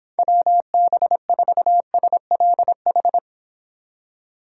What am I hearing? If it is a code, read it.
Morse code, heard as a keyed tone, about 26 wpm, 700 Hz. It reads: WB4SLH